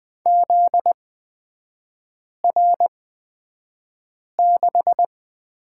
Morse code, heard as a keyed tone, about 20 wpm, spaced as for 5 wpm, 700 Hz. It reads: ZR6
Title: 243 Lesson 39e - Letter z - Three Characters 20wpm